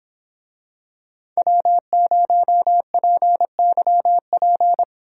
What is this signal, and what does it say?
Morse code, heard as a keyed tone, about 26 wpm, 700 Hz. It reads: W0PYP